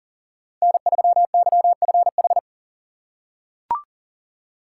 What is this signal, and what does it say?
Morse code, heard as a keyed tone, about 40 wpm, 700 Hz. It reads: N3YFH
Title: USA Call Signs - 5 Characters - 4x Character Spacing - 40wpm